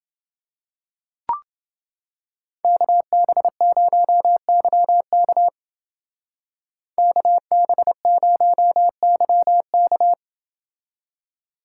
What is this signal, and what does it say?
Morse code, heard as a keyed tone, about 29 wpm, 700 Hz. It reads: KB0YK KB0YK